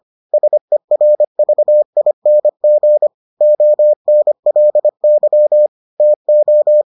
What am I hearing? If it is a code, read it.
Morse code, heard as a keyed tone, about 25 wpm, 600 Hz. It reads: SERVING ONLY TO